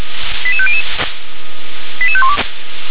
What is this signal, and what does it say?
Five-tone selective call (ZVEI), heard as quick runs of five tones, tones of about 70 ms. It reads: 88500 88512